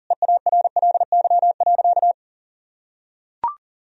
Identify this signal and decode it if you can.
Morse code, heard as a keyed tone, about 40 wpm, 700 Hz. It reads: EARLY.
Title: Sentences from Top 1000 Words - Review Part 2 - 40wpm